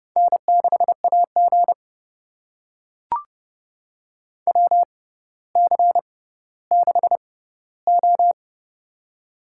Morse code, heard as a keyed tone, about 31 wpm, spaced as for 15 wpm, 700 Hz. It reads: N6AG WC6O
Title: USA Call Signs - 4 Characters - 6x Character Spacing - 30wpm